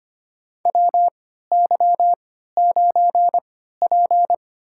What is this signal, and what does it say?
Morse code, heard as a keyed tone, about 25 wpm, spaced as for 14 wpm, 700 Hz. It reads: WY9P
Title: USA Call Signs - 4 Characters - 3x Character Spacing - 25wpm